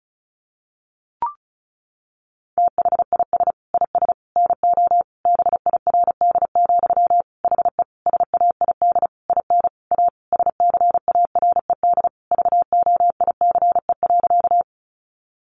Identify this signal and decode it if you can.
Morse code, heard as a keyed tone, about 34 wpm, 700 Hz. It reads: THIS IS NO BIRD, HE SAID IN A SCARED VOICE.
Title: Sentences from Top 700 Words 35wpm